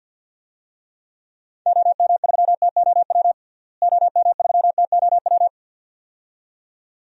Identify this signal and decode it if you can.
Morse code, heard as a keyed tone, about 46 wpm, 700 Hz. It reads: OM3TOW OM3TOW
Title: DXCC-100 Call Signs - 6 Characters - Repeated 2x - 50wpm